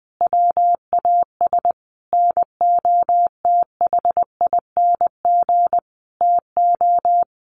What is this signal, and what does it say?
Morse code, heard as a keyed tone, about 20 wpm, 700 Hz. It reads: WAS NOTHING TO
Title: Sentences from Top 1000 Words 20wpm